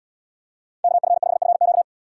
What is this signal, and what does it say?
Morse code, heard as a keyed tone, about 78 wpm, 700 Hz. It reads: KF5FQ